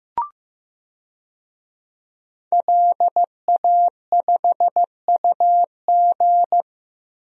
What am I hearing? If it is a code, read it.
Morse code, heard as a keyed tone, about 15 wpm, 700 Hz. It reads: LA5UG
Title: DXCC-100 Call Signs - 5 Characters - Repeated 2x - 15wpm